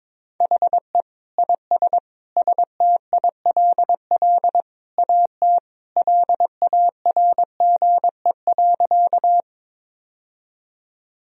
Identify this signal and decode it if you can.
Morse code, heard as a keyed tone, about 22 wpm, 700 Hz. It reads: HE IS STILL AT LARGE.
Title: Sentences from Top 1000 Words - Review Part 1 - 22wpm